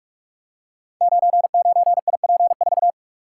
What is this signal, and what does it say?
Morse code, heard as a keyed tone, about 43 wpm, 700 Hz. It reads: 99IP4
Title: Code Groups - 5 Letter:Numbers - 45wpm